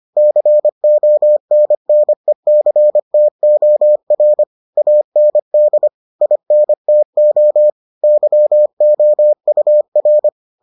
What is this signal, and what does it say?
Morse code, heard as a keyed tone, about 25 wpm, 600 Hz. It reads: CONNECTOR AND INTO YOUR